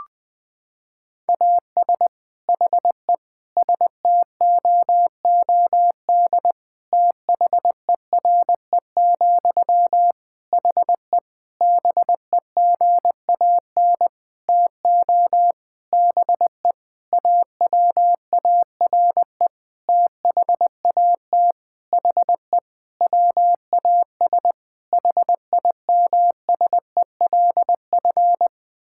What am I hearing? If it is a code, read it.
Morse code, heard as a keyed tone, about 20 wpm, 700 Hz. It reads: AS HE STOOD THERE, HE BEGAN TO BE AWARE THAT HE WAS HIMSELF
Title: Sentences from Top 1600 Words 20wpm